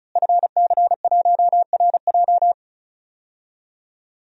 Morse code, heard as a keyed tone, about 34 wpm, 700 Hz. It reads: FC1RJ
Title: DXCC-100 Call Signs - 5 Characters - Repeated 2x - 35wpm